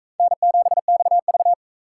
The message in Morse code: N7XV

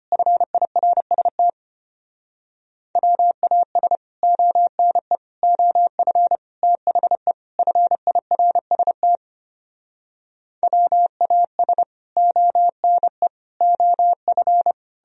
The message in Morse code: FIRST WAS ONE OF THE FIRST WAS ONE OF

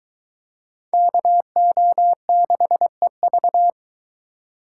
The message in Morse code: KO6EV

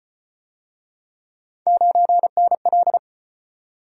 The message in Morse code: 9NL

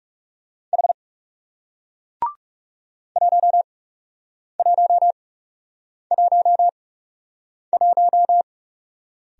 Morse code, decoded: H 1111

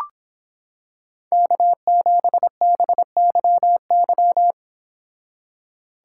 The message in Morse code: K7BYY

